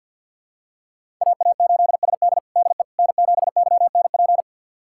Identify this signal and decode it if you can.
Morse code, heard as a keyed tone, about 52 wpm, 700 Hz. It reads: AA8SD DE N7YNP
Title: USA Callsign DE Callsign - 5 Characters - 50wpm